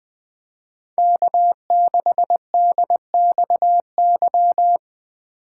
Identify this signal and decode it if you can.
Morse code, heard as a keyed tone, about 20 wpm, 700 Hz. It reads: K6DXY
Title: Speed Racing - USA 1x3 Call Signs 17wpm